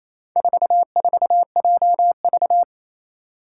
Morse code, heard as a keyed tone, about 28 wpm, 700 Hz. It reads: 44JV
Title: Four Challenging Letter-Number Combinations 28wpm